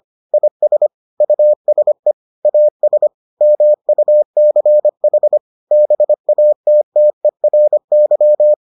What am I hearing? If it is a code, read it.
Morse code, heard as a keyed tone, about 25 wpm, 600 Hz. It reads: IS USE AS MUCH BATTERY